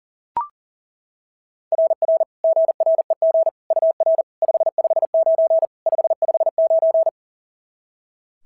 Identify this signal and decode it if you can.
Morse code, heard as a keyed tone, about 37 wpm, 660 Hz. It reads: RR GREG UR 559 559
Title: SOTA QSO Practice 40wpm